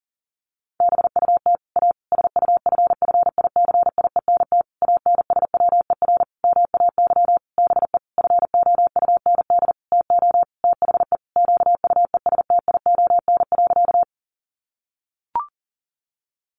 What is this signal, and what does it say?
Morse code, heard as a keyed tone, about 37 wpm, 700 Hz. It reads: BUT A SUFFICIENT ANSWER MAY BE FOUND TO THE QUESTION.
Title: Sentences from Top 1400 Words 40wpm